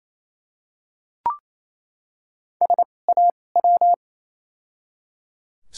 Morse code, heard as a keyed tone, about 28 wpm, spaced as for 20 wpm, 700 Hz. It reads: SAW
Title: 2x Character Spacing - Top 300 Words - 28wpm